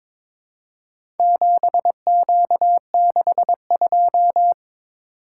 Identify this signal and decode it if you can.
Morse code, heard as a keyed tone, about 22 wpm, 700 Hz. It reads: 7Q62
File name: practice / 22wpm / Four Letter-Number Combinations - 22wpm